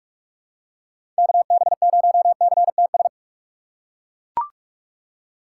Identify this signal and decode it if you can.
Morse code, heard as a keyed tone, about 42 wpm, 690 Hz. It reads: KB0CTS
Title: USA Call Signs - 6 Characters - 2x Character Spacing - 45wpm